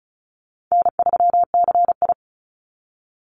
Morse code, heard as a keyed tone, about 34 wpm, 700 Hz. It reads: N3CI
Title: Call Signs - 35wpm